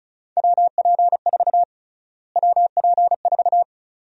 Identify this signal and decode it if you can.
Morse code, heard as a keyed tone, about 36 wpm, 700 Hz. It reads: WP4 WP4
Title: USA Call Sign Prefixes - Repeated 2x - 35wpm